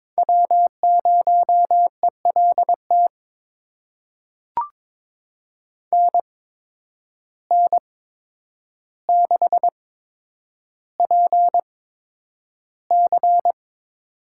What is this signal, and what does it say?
Morse code, heard as a keyed tone, about 22 wpm, spaced as for 9 wpm, 700 Hz. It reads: W0ELT NN6PC
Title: USA Call Signs - 8x Character Spacing - 22wpm